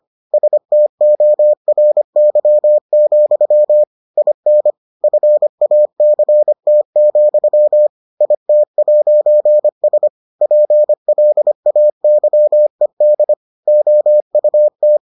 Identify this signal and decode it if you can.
Morse code, heard as a keyed tone, about 25 wpm, 600 Hz. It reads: STORY, IN FACT, IT'S PLAYED OUT